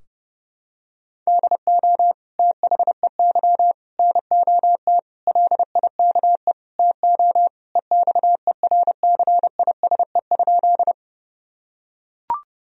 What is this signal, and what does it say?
Morse code, heard as a keyed tone, about 30 wpm, 700 Hz. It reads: DO THEY NOT LIKE TO EXERCISE?
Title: Sentences from Top 1000 Words - Review Part 4 - 30wpm